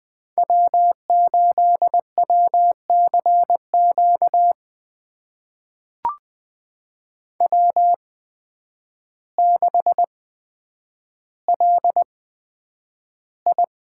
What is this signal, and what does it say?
Morse code, heard as a keyed tone, about 20 wpm, spaced as for 9 wpm, 700 Hz. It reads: W8WCQ W6LI
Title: USA Call Signs - 5 Characters - 8x Character Spacing - 20wpm